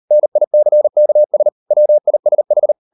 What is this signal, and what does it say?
Morse code, heard as a keyed tone, about 40 wpm, 600 Hz. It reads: NICKS WISH